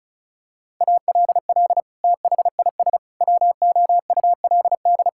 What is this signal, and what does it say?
Morse code, heard as a keyed tone, about 34 wpm, 700 Hz. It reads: ALL THIS WOULD
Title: Sentences from Top 1000 Words - Review Part 6 - 35wpm